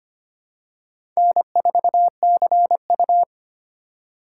Morse code, heard as a keyed tone, about 25 wpm, 700 Hz. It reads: N4CU
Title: USA Call Signs - 3x Character Spacing - 25wpm